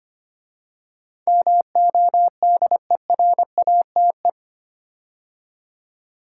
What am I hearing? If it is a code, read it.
Morse code, heard as a keyed tone, about 25 wpm, 700 Hz. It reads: MODERATE
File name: 8 Letter Words - First 3 Letters Given - 25wpm